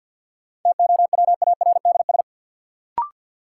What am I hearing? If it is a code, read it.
Morse code, heard as a keyed tone, about 53 wpm, 700 Hz. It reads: TOWARDS
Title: Sentences from Top 1000 Words - Review Part 2 - New Word Drill - 50wpm